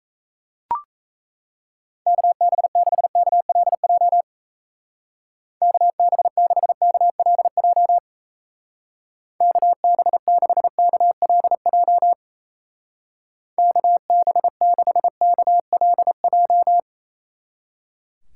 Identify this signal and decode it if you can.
Morse code, heard as a keyed tone, about 34 wpm, 700 Hz. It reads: KB6KLJ KB6KLJ KB6KLJ KB6KLJ
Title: Speed Racing - USA 2x3 Call Signs 28wpm